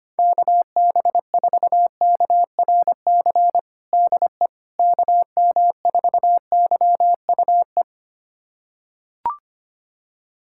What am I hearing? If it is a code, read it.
Morse code, heard as a keyed tone, about 25 wpm, 700 Hz. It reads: KB4KRC DE KM4YUE
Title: USA Callsign DE Callsign - 6 Characters - 25wpm